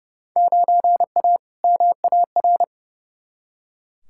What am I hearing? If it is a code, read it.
Morse code, heard as a keyed tone, about 30 wpm, 700 Hz. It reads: 9A MAR
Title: Field Day Exchanges Encore - 30wpm